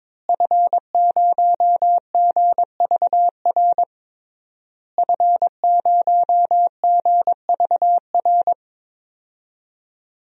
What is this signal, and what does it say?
Morse code, heard as a keyed tone, about 22 wpm, 700 Hz. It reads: F0GVR F0GVR
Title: DXCC-100 Call Signs - 5 Characters - Repeated 3x - 22wpm